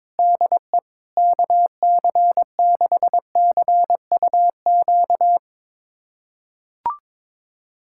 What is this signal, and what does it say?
Morse code, heard as a keyed tone, about 22 wpm, 700 Hz. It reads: DE KC6CUQ
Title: USA Callsign DE Callsign 22wpm